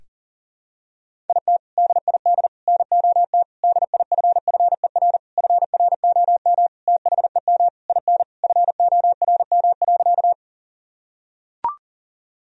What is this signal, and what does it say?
Morse code, heard as a keyed tone, about 37 wpm, 700 Hz. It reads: IT DID NOT DIFFER FROM THEM IN FORM.
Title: Sentences from Top 1000 Words - Review Part 1 - 40wpm